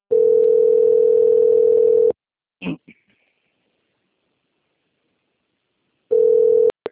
Ringback tone (North America), 440 + 480 Hz.